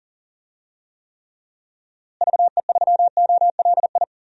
Morse code, heard as a keyed tone, about 40 wpm, 700 Hz. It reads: VE3OLI
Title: DXCC-100 Call Signs - 6 Characters - Repeated 2x - 40wpm